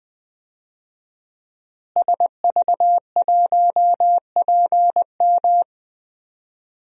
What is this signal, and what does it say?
Morse code, heard as a keyed tone, about 21 wpm, 700 Hz. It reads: SV1PM
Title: DXCC-100 Call Signs - 5 Characters - Repeated 3x - 20wpm